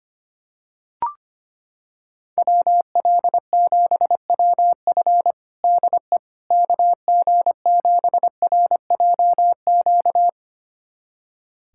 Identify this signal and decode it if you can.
Morse code, heard as a keyed tone, about 25 wpm, 700 Hz. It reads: WL7WF DE KG7RJQ
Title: USA Callsign DE Callsign 25wpm